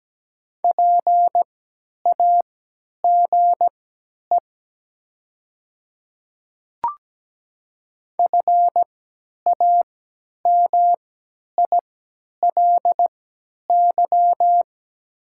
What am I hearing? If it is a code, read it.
Morse code, heard as a keyed tone, about 17 wpm, spaced as for 10 wpm, 700 Hz. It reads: PAGE FAMILY
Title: 3x Character Spacing - Top 500 Words - 17wpm